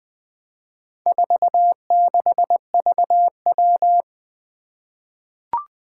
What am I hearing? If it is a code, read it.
Morse code, heard as a keyed tone, about 21 wpm, 700 Hz. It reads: 46VW
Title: Four Challenging Letter-Number Combinations 20wpm